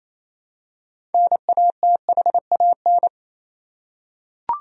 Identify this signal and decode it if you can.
Morse code, heard as a keyed tone, about 28 wpm, 700 Hz. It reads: NATHAN
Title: USA Common Names - 28wpm